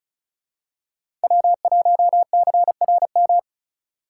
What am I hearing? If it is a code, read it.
Morse code, heard as a keyed tone, about 34 wpm, 700 Hz. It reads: W1CRM
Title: USA Call Signs - 2x Character Spacing - 35wpm